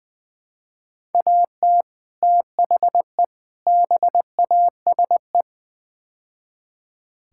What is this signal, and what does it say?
Morse code, heard as a keyed tone, about 20 wpm, 700 Hz. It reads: AT THE BASE
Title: Build A Phrase - 5-Word Phrases - 20wpm